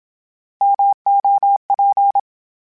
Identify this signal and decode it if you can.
Morse code, heard as a keyed tone, about 27 wpm, 800 Hz. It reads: MOP
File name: Misc / Misc-mp3s / Misc @25